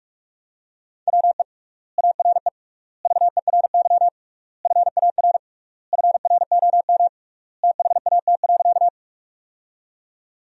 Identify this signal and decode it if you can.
Morse code, heard as a keyed tone, about 46 wpm, 700 Hz. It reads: WE ARE VERY FAR FROM THAT.
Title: 3x Word-Spacing - Sentences from Top 300 Words - 45wpm